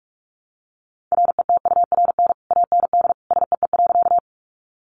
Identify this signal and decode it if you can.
Morse code, heard as a keyed tone, about 44 wpm, 700 Hz. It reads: RETURN AND SEE.
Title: Sentences from Top 800 Words 45wpm